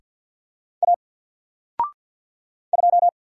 Morse code, heard as a keyed tone, about 48 wpm, 700 Hz. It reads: A 2
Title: Single Character (Letter:Number:Punctuation:Prosign) 50wpm